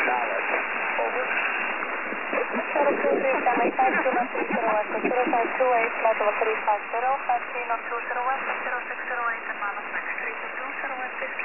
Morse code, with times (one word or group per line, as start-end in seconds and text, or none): none